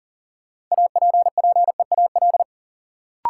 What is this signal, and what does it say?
Morse code, heard as a keyed tone, about 37 wpm, 700 Hz. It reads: APPEAL
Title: Sentences from Top 1300 Words - New Word Drill - 40wpm